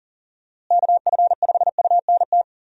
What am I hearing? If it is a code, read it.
Morse code, heard as a keyed tone, about 40 wpm, 700 Hz. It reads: KF5UNT
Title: USA Call Signs - 6 Characters - Repeated 3x - 40wpm